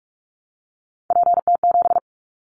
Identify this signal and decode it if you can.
Morse code, heard as a keyed tone, about 44 wpm, 700 Hz. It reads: PT7
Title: Callsign Prefixes for Top 100 DXCC - 45wpm